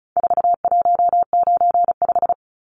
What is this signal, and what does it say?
Morse code, heard as a keyed tone, about 34 wpm, 700 Hz. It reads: 4195
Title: Numbers - 4-Digit 35wpm